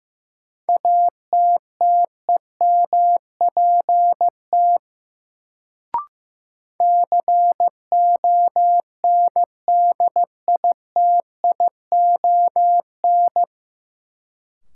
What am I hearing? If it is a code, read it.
Morse code, heard as a keyed tone, about 16 wpm, 700 Hz. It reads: ATTEMPT CONDITION
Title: Sentences from Top 1000 Words - New Word Drill - 15wpm